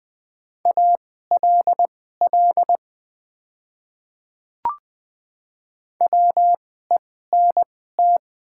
Morse code, heard as a keyed tone, about 20 wpm, spaced as for 14 wpm, 700 Hz. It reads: ALL WENT
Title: 2x Character Spacing - Top 300 Words - 20wpm